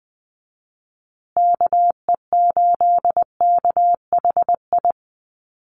Morse code, heard as a keyed tone, about 20 wpm, 700 Hz. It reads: KE8KHI
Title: USA Call Signs Repeated 2x 20wpm